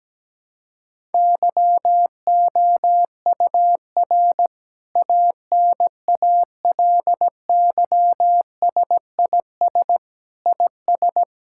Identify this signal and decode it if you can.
Morse code, heard as a keyed tone, about 17 wpm, 700 Hz. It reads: YOUR ANALYSIS IS
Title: Sentences from Top 1000 Words - Review Part 5 - 17wpm